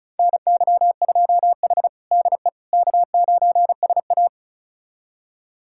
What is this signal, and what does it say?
Morse code, heard as a keyed tone, about 36 wpm, 700 Hz. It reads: NY2H DE K9SA
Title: USA Callsign DE Callsign - 4 Characters - 35wpm